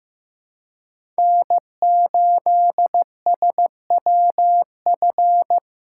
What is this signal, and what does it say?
Morse code, heard as a keyed tone, about 15 wpm, 700 Hz. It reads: N8SWF
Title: USA Call Signs - 5 Characters - 3x Character Spacing - 15wpm